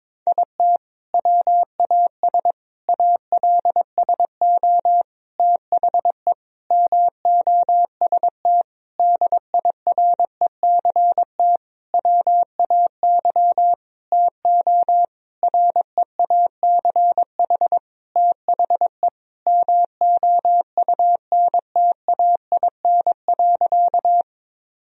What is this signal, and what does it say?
Morse code, heard as a keyed tone, about 22 wpm, 700 Hz. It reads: IT WAS ALSO THE MOST DIRECT WAY TO REACH THE MOUNTAIN.
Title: Sentences from Top 1000 Words - Review Part 2 - 22wpm